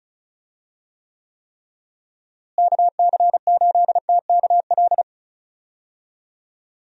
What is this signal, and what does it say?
Morse code, heard as a keyed tone, about 34 wpm, 700 Hz. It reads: KC8TKL
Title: USA Call Signs - 6 Characters - Repeated 3x - 35wpm